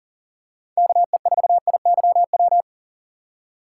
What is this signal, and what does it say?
Morse code, heard as a keyed tone, about 40 wpm, 700 Hz. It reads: KE4IYW
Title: USA Call Signs - 6 Characters - 3x Character Spacing - 40wpm